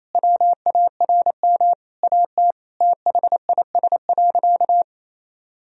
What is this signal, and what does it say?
Morse code, heard as a keyed tone, about 28 wpm, 700 Hz. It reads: WARM AT THIS.
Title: Sentences from Top 1000 Words - Review Part 2 - 28wpm